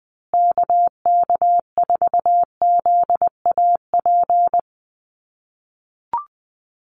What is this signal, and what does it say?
Morse code, heard as a keyed tone, about 20 wpm, 700 Hz. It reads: KK4ZAP
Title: USA Call Signs Repeated 2x 20wpm